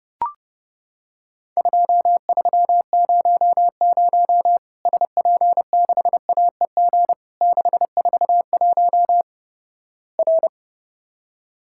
Morse code, heard as a keyed tone, about 32 wpm, 700 Hz. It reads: 2300 SP6AEG 641 R
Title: Worked All Europe DX Contest - QTC Exchange - 30wpm